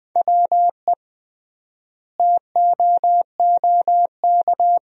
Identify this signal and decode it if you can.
Morse code, heard as a keyed tone, about 20 wpm, 700 Hz. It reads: WE TOOK
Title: 3x Word-Spacing - Sentences from Top 400 Words - 20wpm